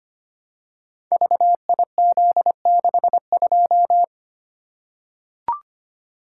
Morse code, heard as a keyed tone, about 25 wpm, 700 Hz. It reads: VIZ62
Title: Code Groups - 5 Letter:Numbers - 25wpm